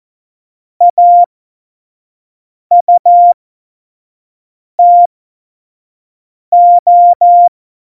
Morse code, heard as a keyed tone, about 14 wpm, spaced as for 5 wpm, 700 Hz. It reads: AUTO
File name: Morsekurs / Witze / Witz 19 Mai 2020, 25 F70